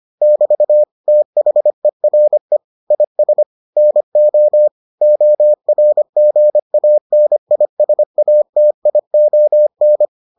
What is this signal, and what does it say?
Morse code, heard as a keyed tone, about 25 wpm, 600 Hz. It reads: <BT> THERE IS NO ORGANISATION